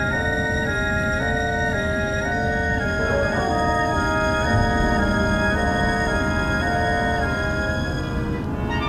There is no code.